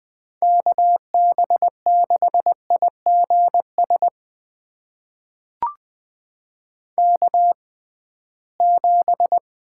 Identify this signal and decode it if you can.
Morse code, heard as a keyed tone, about 20 wpm, spaced as for 15 wpm, 700 Hz. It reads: KB6IGS K7
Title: USA Call Signs - 6x Character Spacing - 20wpm